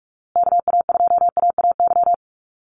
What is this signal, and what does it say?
Morse code, heard as a keyed tone, about 44 wpm, 690 Hz. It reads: KA2AAY
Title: USA Call Signs Repeated 3x 45wpm